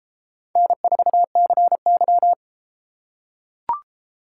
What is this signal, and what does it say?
Morse code, heard as a keyed tone, about 32 wpm, 700 Hz. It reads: N4CY